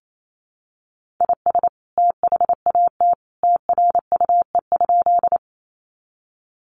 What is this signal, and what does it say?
Morse code, heard as a keyed tone, about 28 wpm, 700 Hz. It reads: IS THAT TRUE?
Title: Sentences from Top 400 Words 28wpm